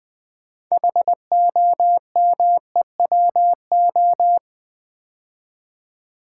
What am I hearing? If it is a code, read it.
Morse code, heard as a keyed tone, about 20 wpm, 700 Hz. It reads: HOMEWO
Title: Build A Word - 2 Letters At A Time - 8-Letter Words - 20wpm